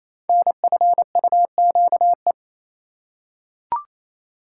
Morse code, heard as a keyed tone, about 27 wpm, 690 Hz. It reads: NFUQE
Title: Code Groups - 5 Letter:Numbers - 28wpm